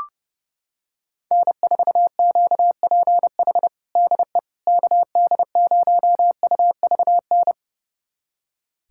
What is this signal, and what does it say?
Morse code, heard as a keyed tone, about 30 wpm, 700 Hz. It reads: N4QPH DE KD0UVN
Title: USA Callsign DE Callsign 30wpm